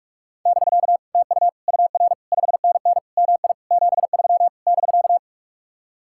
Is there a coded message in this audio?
Yes: Morse code from 0.5 to 5.2 s, heard as a keyed tone, about 49 wpm, 700 Hz. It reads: <BK> TU UR 5NN MI 73 <BK>